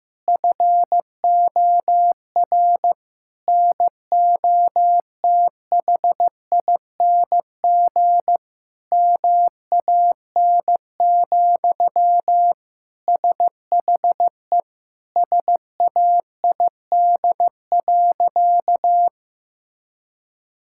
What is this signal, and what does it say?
Morse code, heard as a keyed tone, about 15 wpm, 700 Hz. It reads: FOR NOTHING MAN, SHE SAID.
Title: Sentences from Top 1000 Words - Review Part 6 - 15wpm